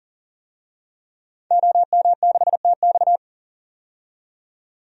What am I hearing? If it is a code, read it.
Morse code, heard as a keyed tone, about 40 wpm, 700 Hz. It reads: OM6TX
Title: DXCC-100 Call Signs - 5 Characters - 40wpm